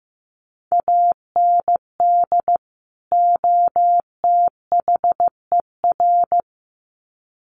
Morse code, heard as a keyed tone, about 15 wpm, 700 Hz. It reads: AND OTHER
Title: Sets of 2 Words 15wpm